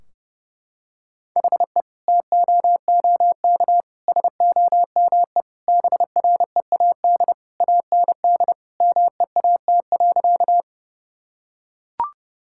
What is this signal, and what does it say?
Morse code, heard as a keyed tone, about 30 wpm, 700 Hz. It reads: HE TOOK SOME BREAD AND MEAT.
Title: Sentences from Top 1600 Words 30wpm